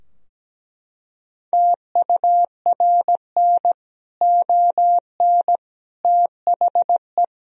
Morse code, heard as a keyed tone, about 17 wpm, 700 Hz. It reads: TURN ON THE
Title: Sentences from Top 1000 Words - Review Part 3 - 17wpm